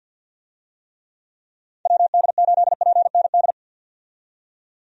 Morse code, heard as a keyed tone, about 51 wpm, 700 Hz. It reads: WD8PND
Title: USA Call Signs - 6 Characters - 50wpm